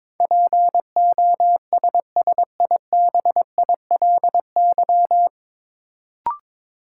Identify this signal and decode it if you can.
Morse code, heard as a keyed tone, about 22 wpm, 700 Hz. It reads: POSSIBILY
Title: Sentences from Top 1300 Words - New Word Drill - 22wpm